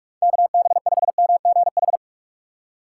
Morse code, heard as a keyed tone, about 46 wpm, 690 Hz. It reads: KB5MGH
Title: USA Call Signs - 6 Characters - 3x Character Spacing - 45wpm